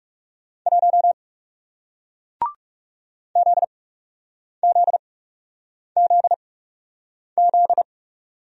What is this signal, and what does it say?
Morse code, heard as a keyed tone, about 40 wpm, spaced as for 8 wpm, 700 Hz. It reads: 1 ZZZZ